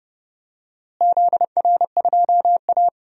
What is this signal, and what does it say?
Morse code, heard as a keyed tone, about 31 wpm, 700 Hz. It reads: ZR2A
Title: DXCC-100 Call Signs - 4 Characters - Repeated 3x - 30wpm